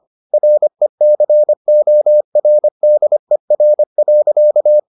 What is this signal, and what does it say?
Morse code, heard as a keyed tone, about 25 wpm, 600 Hz. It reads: RECORDER.